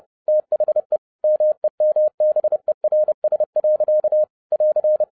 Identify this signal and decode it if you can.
Morse code, heard as a keyed tone, about 30 wpm, 610 Hz. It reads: THE MEMBERS. <AR>
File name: 30wpm